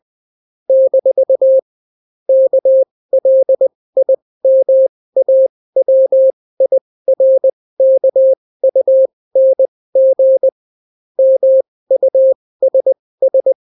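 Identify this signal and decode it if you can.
Morse code, heard as a keyed tone, about 20 wpm, spaced as for 16 wpm, 530 Hz. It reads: - KLIMAWIRKUNG MUSS
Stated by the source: Daily Morse weather and top stories at 16 wpm, clean without QRN.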